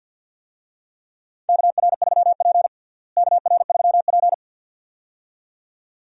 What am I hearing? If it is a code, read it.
Morse code, heard as a keyed tone, about 52 wpm, 700 Hz. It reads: KR3P KR3P
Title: USA Call Signs - 4 Characters - Repeated 2x - 50wpm